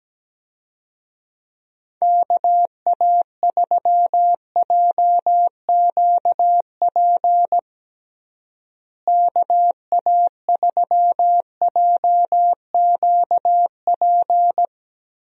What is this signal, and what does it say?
Morse code, heard as a keyed tone, about 17 wpm, 700 Hz. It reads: KA3JQP KA3JQP
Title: USA Call Signs - 6 Characters - Repeated 3x - 17wpm